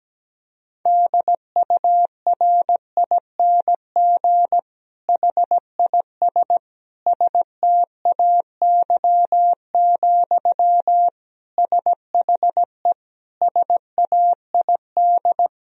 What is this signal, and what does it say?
Morse code, heard as a keyed tone, about 17 wpm, 700 Hz. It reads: DURING HIS STAY, SHE SAID
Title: Sentences from Top 1000 Words - Review Part 2 - 17wpm